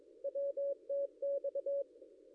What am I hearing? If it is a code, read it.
Morse code, heard as a keyed tone, about 22 wpm, 560 Hz. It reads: WTX
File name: contests / ss-cw-k5tr-op-n5rz -0800